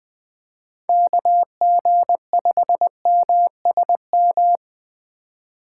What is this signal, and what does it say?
Morse code, heard as a keyed tone, about 20 wpm, 700 Hz. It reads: KG5MSM
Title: USA Call Signs - 6 Characters - 2x Character Spacing - 20wpm